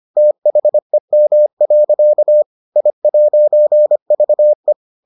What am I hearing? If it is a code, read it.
Morse code, heard as a keyed tone, about 25 wpm, 600 Hz. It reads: THEM. I'VE